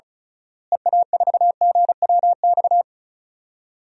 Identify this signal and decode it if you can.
Morse code, heard as a keyed tone, about 36 wpm, 700 Hz. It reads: EA4GWX
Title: DXCC-100 Call Signs - 6 Characters - Repeated 2x - 35wpm